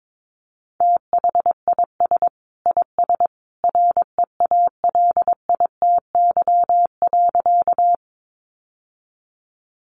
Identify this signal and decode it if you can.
Morse code, heard as a keyed tone, about 22 wpm, 700 Hz. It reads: THIS IS REALITY.
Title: Sentences from Top 1200 Words 22wpm